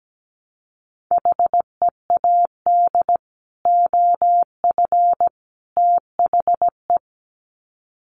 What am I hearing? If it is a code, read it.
Morse code, heard as a keyed tone, about 17 wpm, 700 Hz. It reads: HEAD OF THE
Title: Sets of 3 Words 17wpm